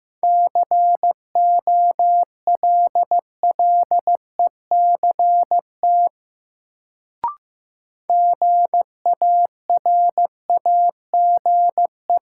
Morse code, heard as a keyed tone, about 15 wpm, 700 Hz. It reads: COLLECT GARAGE
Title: Sentences from Top 1600 Words - New Word Drill Alt - 15wpm